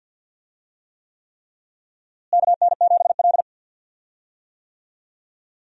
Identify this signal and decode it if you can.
Morse code, heard as a keyed tone, about 47 wpm, 700 Hz. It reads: KN7L
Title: USA Call Signs - 4 Characters - Repeated 3x - 50wpm